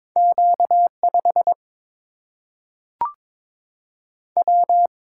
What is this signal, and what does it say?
Morse code, heard as a keyed tone, about 22 wpm, 700 Hz. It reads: Q5 W
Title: Single-Letter Single-Number 22wpm